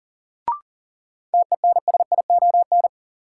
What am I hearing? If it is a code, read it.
Morse code, heard as a keyed tone, about 37 wpm, 700 Hz. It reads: TENSION
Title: Sentences from Top 1600 Words - New Word Drill Alt - 40wpm